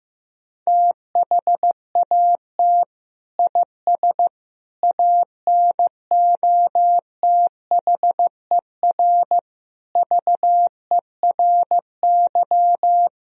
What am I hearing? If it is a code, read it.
Morse code, heard as a keyed tone, about 15 wpm, 700 Hz. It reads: THAT IS ANOTHER VERY
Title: Sentences from Top 1000 Words - Review Part 2 - 15wpm